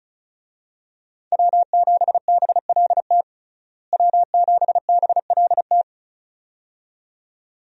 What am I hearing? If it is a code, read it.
Morse code, heard as a keyed tone, about 36 wpm, 700 Hz. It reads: W7BLT W7BLT
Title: USA Call Signs - 5 Characters - Repeated 2x - 35wpm